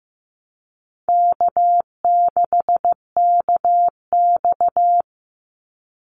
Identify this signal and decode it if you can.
Morse code, heard as a keyed tone, about 15 wpm, 700 Hz. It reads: K6KX
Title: Easy Call Signs Repeated 2x - 15wpm